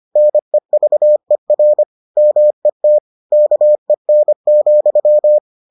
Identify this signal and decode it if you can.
Morse code, heard as a keyed tone, about 25 wpm, 600 Hz. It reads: NEVER MET KEN,